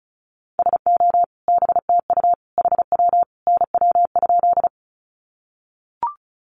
Morse code, heard as a keyed tone, about 34 wpm, 700 Hz. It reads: SO BTU HW NW?
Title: Common QSO Phrases 35wpm